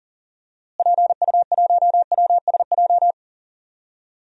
Morse code, heard as a keyed tone, about 40 wpm, 700 Hz. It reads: PU1WSJ
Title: DXCC-100 Call Signs - 6 Characters - Repeated 3x - 40wpm